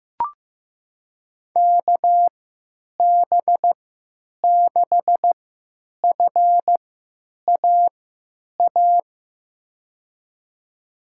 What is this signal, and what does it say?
Morse code, heard as a keyed tone, about 16 wpm, spaced as for 9 wpm, 700 Hz. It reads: KB6FAA
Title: USA Call Signs - 3x Character Spacing - 15wpm